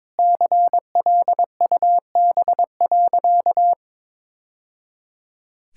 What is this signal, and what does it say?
Morse code, heard as a keyed tone, about 22 wpm, 700 Hz. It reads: CLUB.
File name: practice / 22wpm / Sentences with 3-Word Phrases - 22wpm